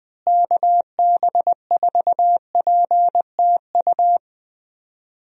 Morse code, heard as a keyed tone, about 20 wpm, 700 Hz. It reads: KB4PTU
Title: USA Call Signs - 6 Characters - Repeated 2x - 20wpm